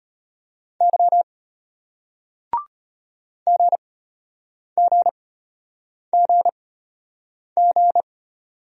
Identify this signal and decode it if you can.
Morse code, heard as a keyed tone, about 34 wpm, spaced as for 8 wpm, 700 Hz. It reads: Y GGGG